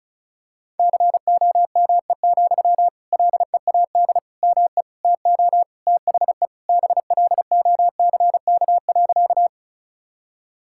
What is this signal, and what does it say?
Morse code, heard as a keyed tone, about 36 wpm, 700 Hz. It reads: COME, LEAD ME TO THE BLOCK.
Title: Sentences from Top 1600 Words 35wpm